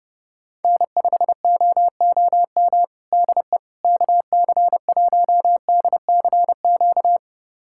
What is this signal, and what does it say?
Morse code, heard as a keyed tone, about 32 wpm, 700 Hz. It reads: N5OOM DE KC1DCQ
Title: USA Callsign DE Callsign 30wpm